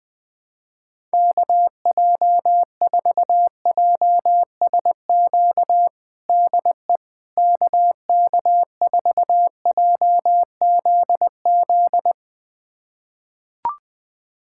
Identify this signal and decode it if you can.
Morse code, heard as a keyed tone, about 20 wpm, 700 Hz. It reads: KJ4JSQ DE KK4JZZ
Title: USA Callsign DE Callsign 20wpm